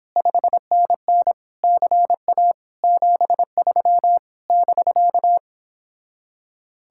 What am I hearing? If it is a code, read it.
Morse code, heard as a keyed tone, about 29 wpm, 700 Hz. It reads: 5NN CA 73 <BK>